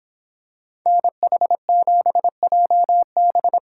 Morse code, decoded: NH7JB

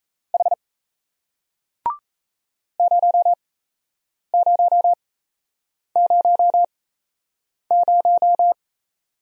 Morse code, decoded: H 0000

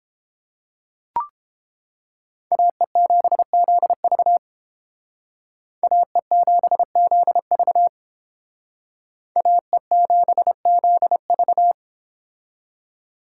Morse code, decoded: AE7ZV AE7ZV AE7ZV

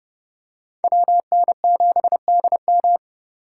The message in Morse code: WN7DM

